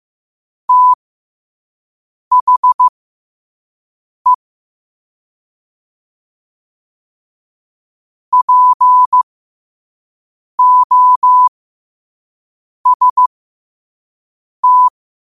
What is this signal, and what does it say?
Morse code, heard as a keyed tone, about 15 wpm, spaced as for 5 wpm, 1000 Hz. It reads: THE POST
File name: qotdfiles / 5wpm